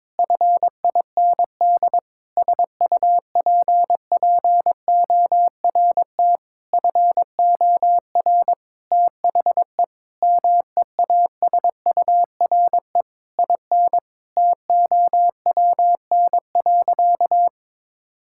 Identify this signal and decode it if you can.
Morse code, heard as a keyed tone, about 22 wpm, 700 Hz. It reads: FIND SUPPORT FOR THE MEASURE IN TOWN.
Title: Sentences from Top 1000 Words - Review Part 5 - 22wpm